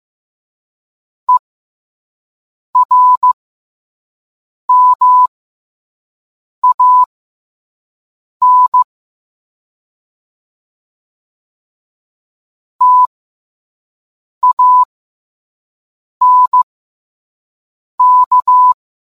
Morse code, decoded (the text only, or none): ERMAN TANK